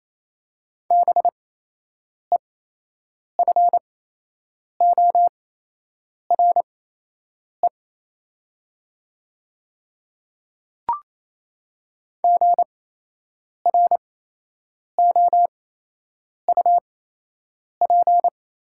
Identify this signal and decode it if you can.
Morse code, heard as a keyed tone, about 28 wpm, spaced as for 8 wpm, 700 Hz. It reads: BEFORE GROUP
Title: 8x Character Spacing - Top 300 Words - 28wpm